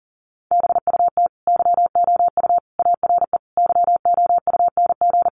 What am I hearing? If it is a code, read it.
Morse code, heard as a keyed tone, about 40 wpm, 700 Hz. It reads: BUT YOU ARE YOUNG
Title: Sentences from Top 1200 Words 40wpm